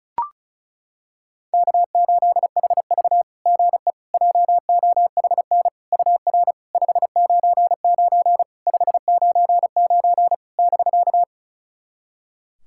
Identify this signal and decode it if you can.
Morse code, heard as a keyed tone, about 36 wpm, 690 Hz. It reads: K8HV GE JOHN UR 599 599 <BK>
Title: SOTA QSO Practice 35wpm